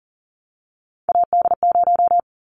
Morse code, heard as a keyed tone, about 40 wpm, 700 Hz. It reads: AD0
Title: Callsign Prefixes for Top 25 DXCC - 40wpm